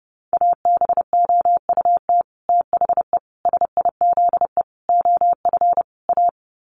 Morse code, heard as a keyed tone, about 30 wpm, 700 Hz. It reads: ABOUT THE SIZE OF A
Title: Sets of 5 Words 30wpm